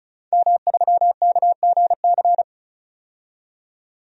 Morse code, heard as a keyed tone, about 36 wpm, 700 Hz. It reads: M3KGC
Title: DXCC-100 Call Signs - 5 Characters - 35wpm